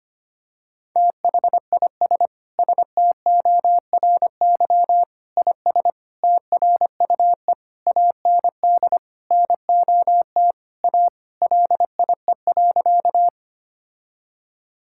Morse code, heard as a keyed tone, about 25 wpm, 700 Hz. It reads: THIS STORY IS TRUE AND NOT A LIE.
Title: Sentences from Top 1600 Words 25wpm